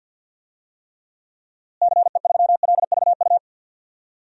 Morse code, heard as a keyed tone, about 51 wpm, 700 Hz. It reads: CE3LVU
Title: DXCC-100 Call Signs - 6 Characters - Repeated 2x - 50wpm